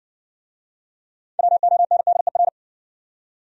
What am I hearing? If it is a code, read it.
Morse code, heard as a keyed tone, about 60 wpm, 700 Hz. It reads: WONDER